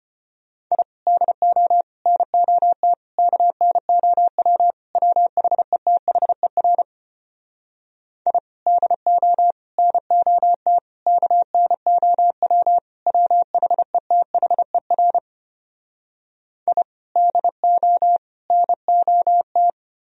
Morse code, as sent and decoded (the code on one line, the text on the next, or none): .. / -.. --- / -. --- - / -.- -. --- .-- / .-- .... - .... .-. / .. / -.. --- / -. --- - / -.- -. --- .-- / .-- .... - .... .-. / .. / -.. --- / -. --- -
I DO NOT KNOW WHETHER I DO NOT KNOW WHETHER I DO NOT